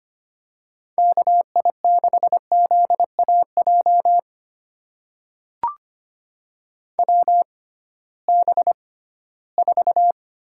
Morse code, heard as a keyed tone, about 25 wpm, spaced as for 16 wpm, 700 Hz. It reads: KI6ZAJ WB4
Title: USA Call Signs - 6 Characters - 6x Character Spacing - 25wpm